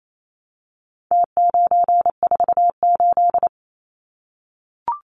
Morse code, heard as a keyed tone, about 28 wpm, 700 Hz. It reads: T948
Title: Numbers - Contest Style - 4-Digit 28wpm